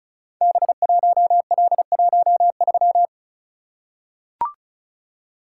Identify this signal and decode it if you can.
Morse code, heard as a keyed tone, about 36 wpm, 690 Hz. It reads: B1L13
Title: Code Groups - 5 Letter:Numbers - 35wpm